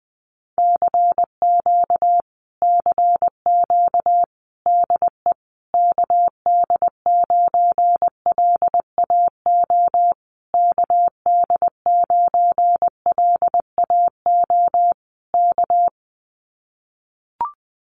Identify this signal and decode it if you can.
Morse code, heard as a keyed tone, about 20 wpm, 700 Hz. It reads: CQ CQ DE KD9LAO KD9LAO K
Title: Common QSO Phrases Encore 20wpm